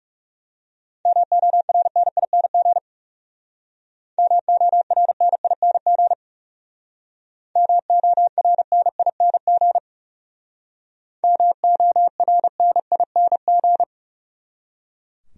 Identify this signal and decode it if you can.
Morse code, heard as a keyed tone, about 37 wpm, 700 Hz. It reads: MORNING MORNING MORNING MORNING
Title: Speed-Racing Top 500 Words 30wpm